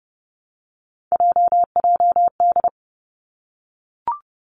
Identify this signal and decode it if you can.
Morse code, heard as a keyed tone, about 30 wpm, 700 Hz. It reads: JJD
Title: Three Challenging Letter-Number Combinations 30wpm